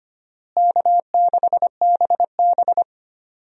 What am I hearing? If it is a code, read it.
Morse code, heard as a keyed tone, about 25 wpm, 700 Hz. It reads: K6BB
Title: USA Call Signs - 4 Characters - Repeated 3x - 25wpm